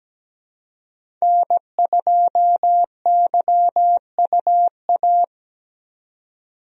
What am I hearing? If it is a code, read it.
Morse code, heard as a keyed tone, about 17 wpm, 700 Hz. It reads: N2YUA
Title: USA Call Signs - 5 Characters - 17wpm